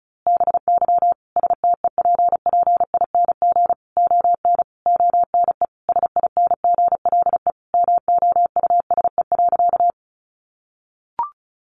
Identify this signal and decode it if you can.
Morse code, heard as a keyed tone, about 34 wpm, 700 Hz. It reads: BY STEPPING ON ONE SINGLE MOUSE.
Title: A Sound Of Thunder 35wpm